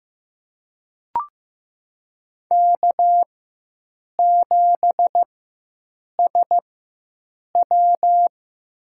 Morse code, heard as a keyed tone, about 15 wpm, spaced as for 7 wpm, 700 Hz. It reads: K7SW
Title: USA Call Signs - 4x Character Spacing - 15wpm